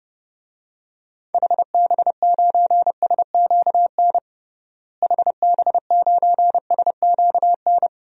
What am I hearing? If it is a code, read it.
Morse code, heard as a keyed tone, about 32 wpm, 700 Hz. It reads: HB9SQN HB9SQN
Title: DXCC-100 Call Signs - 6 Characters - Repeated 2x - 30wpm